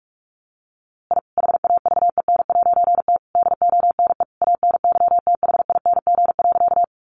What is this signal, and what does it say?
Morse code, heard as a keyed tone, about 44 wpm, 700 Hz. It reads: I HAVEN'T DONE ANYTHING.
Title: A Sound Of Thunder 45wpm